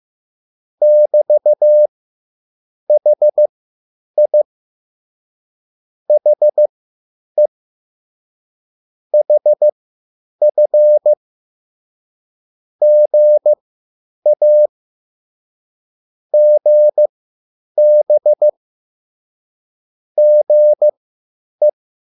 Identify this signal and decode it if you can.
Morse code, heard as a keyed tone, about 15 wpm, spaced as for 8 wpm, 600 Hz. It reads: <BT>HI HE HF GA GB GE
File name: QSL / CW lesson